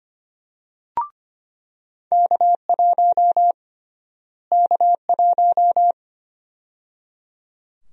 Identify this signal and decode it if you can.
Morse code, heard as a keyed tone, about 25 wpm, 700 Hz. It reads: K1 K1
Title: USA Call Sign Prefixes - Repeated 2x - 25wpm